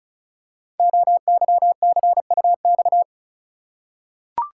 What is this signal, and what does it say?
Morse code, heard as a keyed tone, about 35 wpm, 700 Hz. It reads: OYCUX
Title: Code Groups - 5 Letter:Numbers - 35wpm